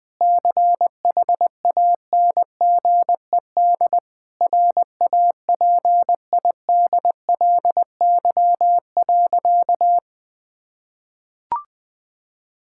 Morse code, heard as a keyed tone, about 21 wpm, 700 Hz. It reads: CHANGED RAPIDLY.
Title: Sentences with 5-Word Phrases - Spoken Beginning - 20wpm